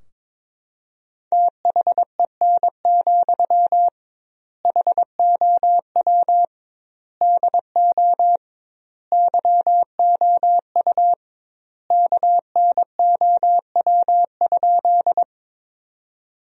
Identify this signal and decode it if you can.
Morse code, heard as a keyed tone, about 21 wpm, 700 Hz. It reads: THEN, HOW DO YOU KNOW?
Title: 2x Word-Spacing - Sentences from Top 100 Words - 22wpm